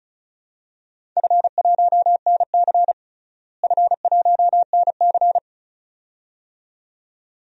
Morse code, heard as a keyed tone, about 36 wpm, 700 Hz. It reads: F1NC F1NC
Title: DXCC-100 Call Signs - 4 Characters - Repeated 2x - 35wpm